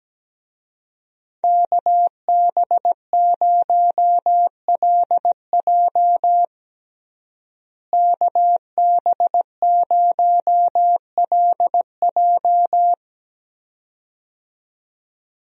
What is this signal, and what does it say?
Morse code, heard as a keyed tone, about 17 wpm, 700 Hz. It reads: KB0LJ KB0LJ
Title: USA Call Signs - 5 Characters - Repeated 2x - 17wpm